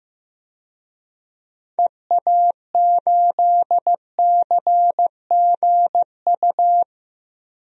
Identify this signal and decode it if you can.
Morse code, heard as a keyed tone, about 15 wpm, 700 Hz. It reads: EA8CGU
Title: DXCC-100 Call Signs - 6 Characters - Repeated 2x - 15wpm